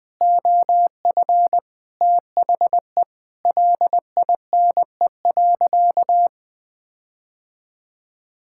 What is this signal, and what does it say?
Morse code, heard as a keyed tone, about 20 wpm, 700 Hz. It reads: OF THE LINE.
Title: Sentences with 5-Word Phrases - Spoken Beginning - 20wpm